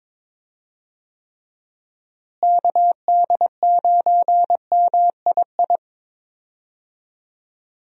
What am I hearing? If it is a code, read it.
Morse code, heard as a keyed tone, about 22 wpm, 700 Hz. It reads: KD9MII
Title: USA Call Signs - 6 Characters - Repeated 2x - 22wpm